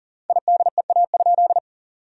Morse code, heard as a keyed tone, about 37 wpm, 700 Hz. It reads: IDEA?
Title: 4x Word-Spacing - Sentences from Top 400 Words - 40wpm